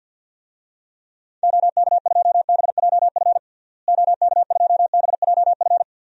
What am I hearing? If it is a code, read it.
Morse code, heard as a keyed tone, about 51 wpm, 700 Hz. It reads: OK2BJF OK2BJF
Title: DXCC-100 Call Signs - 6 Characters - Repeated 3x - 50wpm